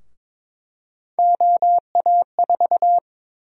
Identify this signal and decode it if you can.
Morse code, heard as a keyed tone, about 22 wpm, 700 Hz. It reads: OA4
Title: USA Call Sign Prefixes - Repeated 2x - 22wpm